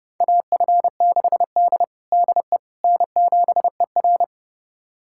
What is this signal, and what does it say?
Morse code, heard as a keyed tone, about 32 wpm, 700 Hz. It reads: AF6D DE N7ER
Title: USA Callsign DE Callsign - 4 Characters - 30wpm